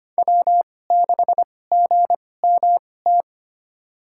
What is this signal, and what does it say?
Morse code, heard as a keyed tone, about 25 wpm, spaced as for 18 wpm, 700 Hz. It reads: W6GMT
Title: USA Call Signs - 2x Character Spacing - 25wpm